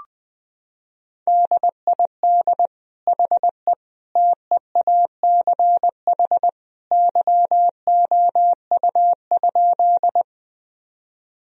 Morse code, heard as a keyed tone, about 20 wpm, 700 Hz. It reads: DID HE TEACH YOU?
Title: Sentences from Top 1000 Words - Review Part 2 - 20wpm